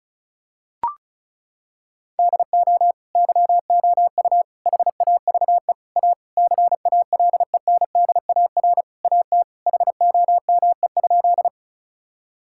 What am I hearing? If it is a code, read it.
Morse code, heard as a keyed tone, about 36 wpm, 700 Hz. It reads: DO YOU HAVE A CALENDAR AT HOME?
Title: Sentences from Top 1600 Words 35wpm